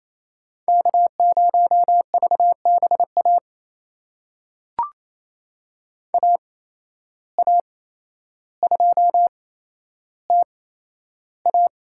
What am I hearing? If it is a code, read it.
Morse code, heard as a keyed tone, about 28 wpm, spaced as for 12 wpm, 700 Hz. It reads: K0VBA AA2TA
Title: USA Call Signs - 5 Characters - 8x Character Spacing - 28wpm